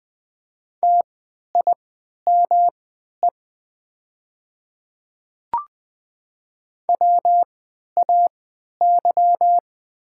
Morse code, heard as a keyed tone, about 20 wpm, spaced as for 11 wpm, 700 Hz. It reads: TIME WAY
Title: 3x Character Spacing - Top 100 Words - 20wpm